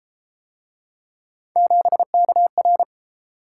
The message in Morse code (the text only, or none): 7KR